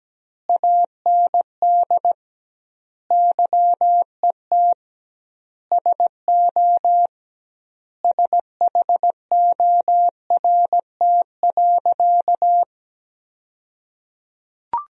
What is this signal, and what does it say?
Morse code, heard as a keyed tone, about 17 wpm, 700 Hz. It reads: AND YET SO SHORT.
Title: 2x Word-Spacing - Sentences from Top 500 Words - 17wpm